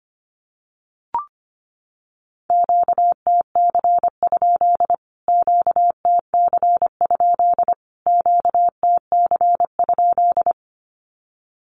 Morse code, heard as a keyed tone, about 25 wpm, 700 Hz. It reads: QTC? QTC? QTC?
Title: Q-Codes Repeated 3x 25wpm